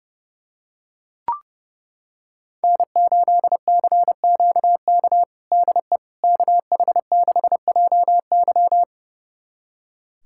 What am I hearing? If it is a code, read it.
Morse code, heard as a keyed tone, about 30 wpm, 700 Hz. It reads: N8CQK DE KH6JY
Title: USA Callsign DE Callsign - 5 Characters - 30wpm